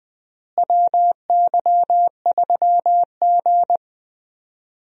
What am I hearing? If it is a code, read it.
Morse code, heard as a keyed tone, about 20 wpm, 700 Hz. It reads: WY3G
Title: USA Call Signs - 4 Characters - Repeated 3x - 20wpm